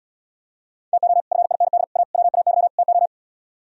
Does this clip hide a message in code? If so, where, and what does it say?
Yes: Morse code from 0.9 to 3.1 s, heard as a keyed tone, about 77 wpm, 700 Hz. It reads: TO FIND A WAY TO